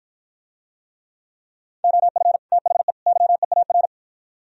Morse code, heard as a keyed tone, about 52 wpm, 700 Hz. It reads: OF THE YEAR